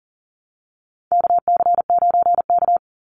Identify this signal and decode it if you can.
Morse code, heard as a keyed tone, about 40 wpm, 700 Hz. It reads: KC9K
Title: Call Signs Repeated 2x - 40wpm